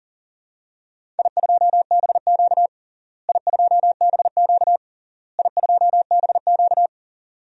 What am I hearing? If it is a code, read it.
Morse code, heard as a keyed tone, about 40 wpm, 700 Hz. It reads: I2BQ I2BQ I2BQ
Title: DXCC-100 Call Signs - 4 Characters - Repeated 3x - 40wpm